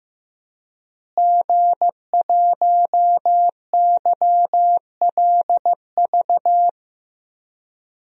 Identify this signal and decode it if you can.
Morse code, heard as a keyed tone, about 15 wpm, 700 Hz. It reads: G1YLV
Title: DXCC-100 Call Signs - 5 Characters - Repeated 3x - 15wpm